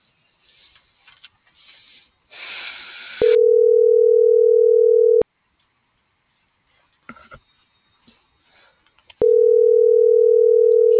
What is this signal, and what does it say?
Ringback tone (North America), 440 + 480 Hz.